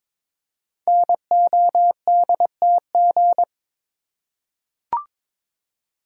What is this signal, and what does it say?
Morse code, heard as a keyed tone, about 22 wpm, 700 Hz. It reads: NODTG
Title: Code Groups - 5 Letters - 22wpm